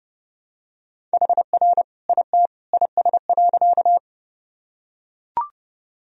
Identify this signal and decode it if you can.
Morse code, heard as a keyed tone, about 30 wpm, 700 Hz. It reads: HR IT IS.
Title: Sentences from Top 200 Words and Abbreviations 30wpm